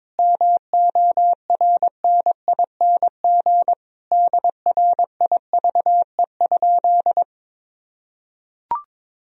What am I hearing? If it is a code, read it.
Morse code, heard as a keyed tone, about 22 wpm, 700 Hz. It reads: MORNING DRIVE?
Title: Sentences from Top 1000 Words - Review Part 3 - 22wpm